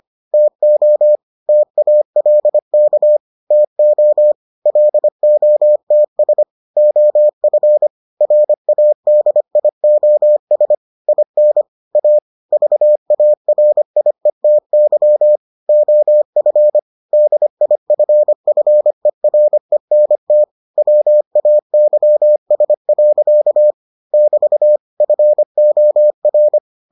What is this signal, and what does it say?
Morse code, heard as a keyed tone, about 25 wpm, 600 Hz. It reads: TO TALK TO LOTS OF RADIOS IN A VARIETY OF DIFFERENT WAYS. <BT> FOR